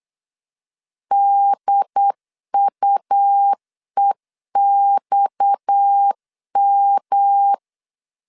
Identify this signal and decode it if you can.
Morse code, heard as a keyed tone, about 8 wpm, 800 Hz. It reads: DUEXM
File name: LEZIONI / 6 PKX0 / 6 PKX 8wpm 3041